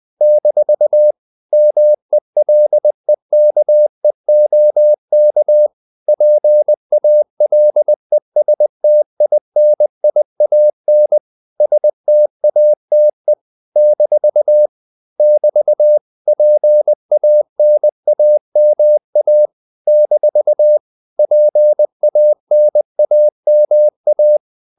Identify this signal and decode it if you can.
Morse code, heard as a keyed tone, about 20 wpm, 600 Hz. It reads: - MELEKEOK PALESTINIAN STATE - <BT> PANAMA - PANAMA
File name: CW / Practice / NationalCapitals20